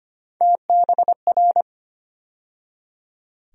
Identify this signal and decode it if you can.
Morse code, heard as a keyed tone, about 25 wpm, 700 Hz. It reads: TBR
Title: USA Call Sign Postfixes - 3 Letters 25wpm